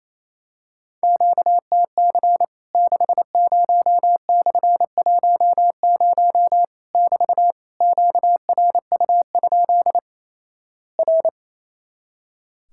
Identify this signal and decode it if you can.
Morse code, heard as a keyed tone, about 28 wpm, 700 Hz. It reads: QTC 60/10 <BT> QRU? R
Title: Worked All Europe DX Contest - QTC Exchange - 28wpm